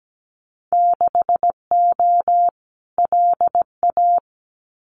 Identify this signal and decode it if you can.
Morse code, heard as a keyed tone, about 17 wpm, 700 Hz. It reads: 6O LA
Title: Winter Field Day Exchanges Encore 17wpm